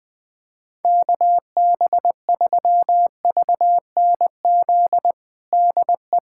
Morse code, heard as a keyed tone, about 20 wpm, 700 Hz. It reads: KB3VNZ DE
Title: USA Callsign DE Callsign 20wpm